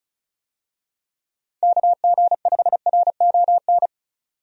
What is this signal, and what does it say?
Morse code, heard as a keyed tone, about 34 wpm, 700 Hz. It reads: KG5RON
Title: USA Call Signs - 6 Characters - 35wpm